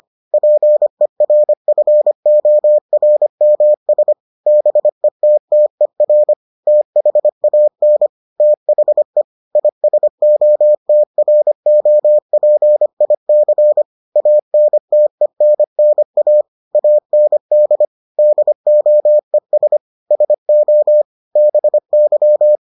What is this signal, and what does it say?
Morse code, heard as a keyed tone, about 25 wpm, 600 Hz. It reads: PERFORMS BETTER THAN THE ISOTROPIC ANTENNA AND DOES SO BY